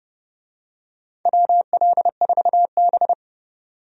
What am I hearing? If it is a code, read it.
Morse code, heard as a keyed tone, about 31 wpm, 700 Hz. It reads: WL4B
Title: USA Call Signs - 4 Characters - 30wpm